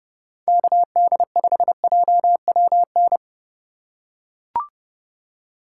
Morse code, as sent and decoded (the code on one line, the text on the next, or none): -.- -.. ..... .--- .-- -.
KD5JWN